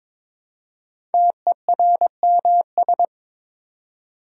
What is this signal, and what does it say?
Morse code, heard as a keyed tone, about 22 wpm, 700 Hz. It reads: TERMS
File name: 5 Letter Words - First 3 Letters Given - 22wpm